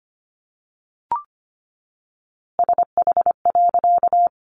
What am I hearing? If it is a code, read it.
Morse code, heard as a keyed tone, about 25 wpm, 700 Hz. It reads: SH.
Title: A Sound Of Thunder 25wpm